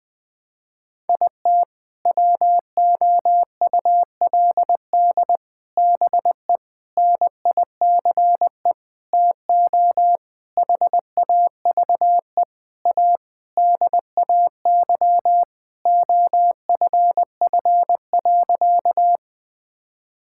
Morse code, heard as a keyed tone, about 20 wpm, 700 Hz. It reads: IT WOULD BE NICE TO HAVE A DAY OFF.
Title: Sentences with 7-Word Phrases - Spoken Beginning - 20wpm